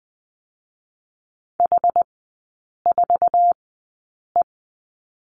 Morse code, heard as a keyed tone, about 20 wpm, spaced as for 8 wpm, 700 Hz. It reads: H4E
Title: 074 Lesson 13e - Letter L - Three Characters 20wpm